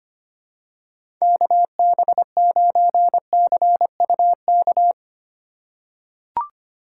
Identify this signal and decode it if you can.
Morse code, heard as a keyed tone, about 25 wpm, 700 Hz. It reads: KB9CUK
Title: USA Call Signs - 6 Characters - 3x Character Spacing - 25wpm